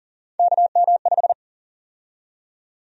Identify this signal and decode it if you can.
Morse code, heard as a keyed tone, about 40 wpm, 700 Hz. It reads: KM5
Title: USA Call Sign Prefixes - 40wpm